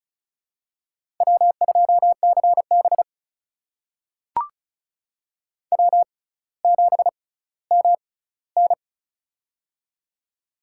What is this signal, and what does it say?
Morse code, heard as a keyed tone, about 36 wpm, spaced as for 18 wpm, 700 Hz. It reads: W2CB W7MN
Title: USA Call Signs - 6x Character Spacing - 35wpm